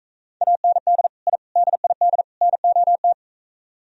Morse code, heard as a keyed tone, about 43 wpm, 700 Hz. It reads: AND I DID NOT